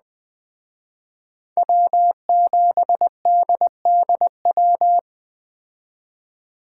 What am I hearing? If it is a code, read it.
Morse code, heard as a keyed tone, about 21 wpm, 700 Hz. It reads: W7DDW